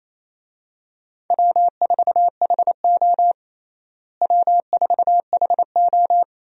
Morse code, heard as a keyed tone, about 28 wpm, 700 Hz. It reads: W4HO W4HO
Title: USA Call Signs - 4 Characters - Repeated 2x - 28wpm